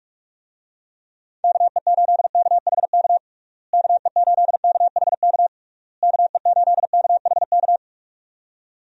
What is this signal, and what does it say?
Morse code, heard as a keyed tone, about 46 wpm, 700 Hz. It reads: KE8KHK KE8KHK KE8KHK
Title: USA Call Signs - 6 Characters - Repeated 3x - 45wpm